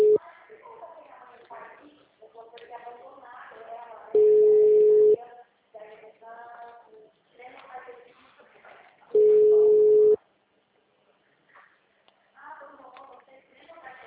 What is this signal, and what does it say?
Ringback tone (Europe, 425 Hz), with speech.